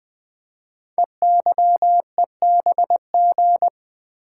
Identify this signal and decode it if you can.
Morse code, heard as a keyed tone, about 20 wpm, 700 Hz. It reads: EYEBG
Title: Code Groups - 5 Letters - 20wpm